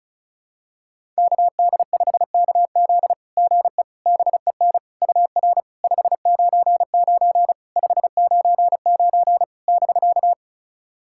Morse code, heard as a keyed tone, about 34 wpm, 690 Hz. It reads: KD5KZ GE BEN UR 599 599 <BK>
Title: SOTA QSO Practice 35wpm